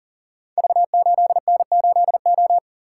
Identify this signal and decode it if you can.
Morse code, heard as a keyed tone, about 42 wpm, 700 Hz. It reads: V8N8O
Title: Code Groups - 5 Letter:Numbers - 40wpm